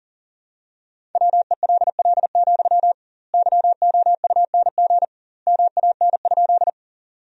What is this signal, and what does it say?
Morse code, heard as a keyed tone, about 40 wpm, 700 Hz. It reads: WELL, YOUNG MAN?
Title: 2x Word-Spacing - Sentences from Top 400 Words - 40wpm